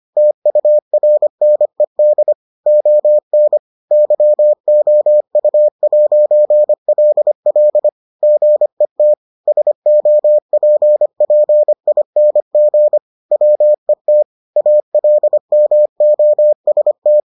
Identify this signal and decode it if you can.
Morse code, heard as a keyed tone, about 25 wpm, 600 Hz. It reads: TURNED ON YOU'LL GET SOPPING WET ALMOST